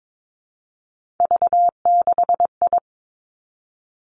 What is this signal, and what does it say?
Morse code, heard as a keyed tone, about 22 wpm, 700 Hz. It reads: V6I
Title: Three Challenging Letter-Number Combinations 22wpm